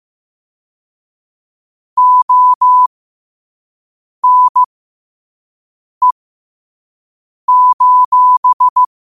Morse code, heard as a keyed tone, about 15 wpm, spaced as for 5 wpm, 1000 Hz. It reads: ONE:
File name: qotdfiles / 5wpm